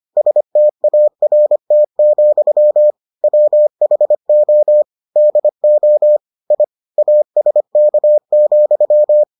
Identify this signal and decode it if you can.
Morse code, heard as a keyed tone, about 25 wpm, 600 Hz. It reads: START, WHO DO I ASK,